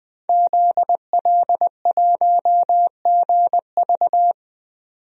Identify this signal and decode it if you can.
Morse code, heard as a keyed tone, about 20 wpm, 700 Hz. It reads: ZL1GV
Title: DXCC-100 Call Signs - 5 Characters - 20wpm